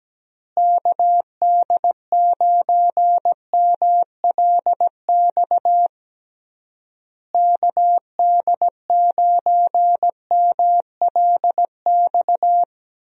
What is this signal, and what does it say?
Morse code, heard as a keyed tone, about 17 wpm, 700 Hz. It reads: KD9MLX KD9MLX
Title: USA Call Signs - 6 Characters - Repeated 3x - 17wpm